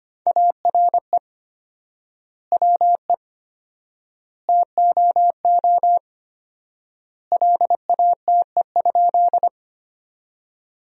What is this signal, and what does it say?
Morse code, heard as a keyed tone, about 25 wpm, 700 Hz. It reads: ARE WE TOO LATE?
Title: 4x Word-Spacing - Sentences from Top 300 Words - 25wpm